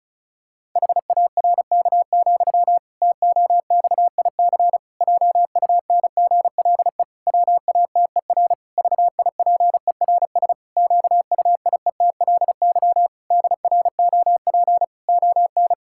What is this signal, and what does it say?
Morse code, heard as a keyed tone, about 36 wpm, 700 Hz. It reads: HARK, TOXIC JUNGLE WATER VIPERS QUIETLY DROP ON
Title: Pangram Challenge v2 - 35wpm